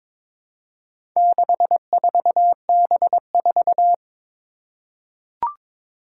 Morse code, heard as a keyed tone, about 22 wpm, 700 Hz. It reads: 64B4
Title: Four Challenging Letter-Number Combinations 22wpm